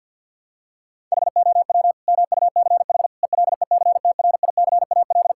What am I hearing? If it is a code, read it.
Morse code, heard as a keyed tone, about 48 wpm, 700 Hz. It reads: HOW MUCH ELECTRICAL
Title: Sentences from Top 1000 Words - Review Part 6 - 50wpm